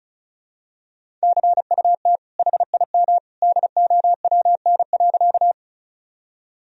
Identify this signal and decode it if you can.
Morse code, heard as a keyed tone, about 36 wpm, 700 Hz. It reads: CUT HIM DOWN.
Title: Sentences from Top 1000 Words - Review Part 1 - 35wpm